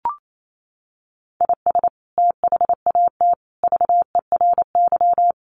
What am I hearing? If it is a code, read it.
Morse code, heard as a keyed tone, about 28 wpm, 700 Hz. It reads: IS THAT VERY
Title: Sentences from Top 1200 Words 28wpm